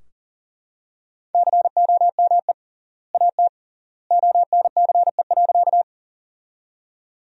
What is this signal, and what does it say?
Morse code, heard as a keyed tone, about 38 wpm, 700 Hz. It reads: COME AT ONCE.
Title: 3x Word-Spacing - Sentences from Top 300 Words - 40wpm